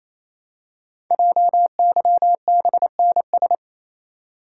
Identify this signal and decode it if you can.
Morse code, heard as a keyed tone, about 27 wpm, 700 Hz. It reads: JYBNS
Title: Code Groups - 5 Letters - 28wpm